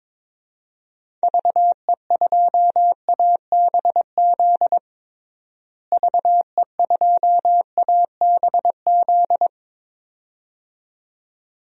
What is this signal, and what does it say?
Morse code, heard as a keyed tone, about 22 wpm, 700 Hz. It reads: VE2ABZ VE2ABZ
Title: DXCC-100 Call Signs - 6 Characters - Repeated 2x - 22wpm